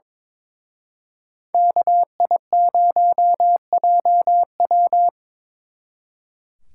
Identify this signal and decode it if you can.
Morse code, heard as a keyed tone, about 26 wpm, 700 Hz. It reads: KI0JW
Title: Speed Racing - USA Call Signs 22wpm